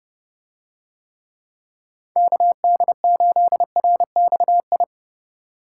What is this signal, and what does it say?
Morse code, heard as a keyed tone, about 29 wpm, 700 Hz. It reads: KD8RXI